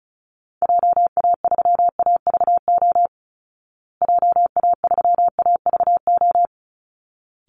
Morse code, heard as a keyed tone, about 34 wpm, 700 Hz. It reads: JA3AVO JA3AVO
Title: Call Signs Repeated 3x - 35wpm